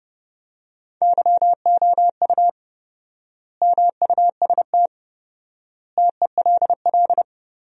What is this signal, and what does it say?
Morse code, heard as a keyed tone, about 31 wpm, 700 Hz. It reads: YOU MUST TELL
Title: 4x Word-Spacing - Sentences from Top 200 Words - 30wpm